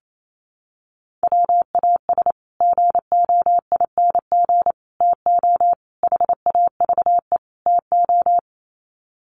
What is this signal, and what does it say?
Morse code, heard as a keyed tone, about 28 wpm, 700 Hz. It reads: WAS GOING TO HAVE TO
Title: Sets of 5 Words Encore 28wpm